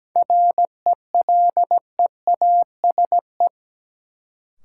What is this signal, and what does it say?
Morse code, heard as a keyed tone, about 17 wpm, 700 Hz. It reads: RELEASE
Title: Sentences from Top 1100 Words - New Word Drill - 17wpm